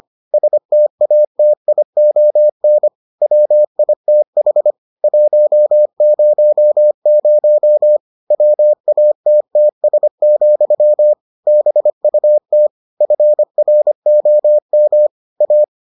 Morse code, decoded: STATION WITH 100 WATTS, BUT FROM A

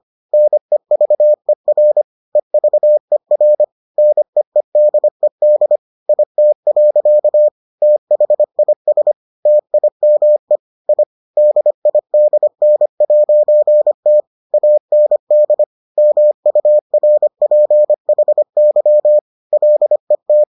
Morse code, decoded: NEVER EVER NEEDED IT. THIS TIME I DIDN'T AND MURPHY LET

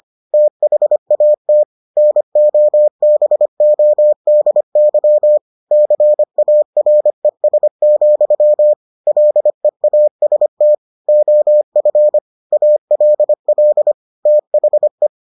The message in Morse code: THAT NOBODY CARES, LEAST OF ALL THE

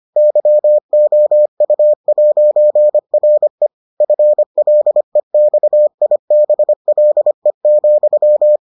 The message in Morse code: YOU'RE FLEXIBLE,

